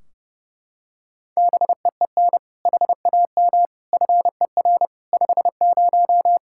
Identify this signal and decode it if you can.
Morse code, heard as a keyed tone, about 30 wpm, 700 Hz. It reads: BEEN HAM FER 50
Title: K1USN SST QSO Practice 30wpm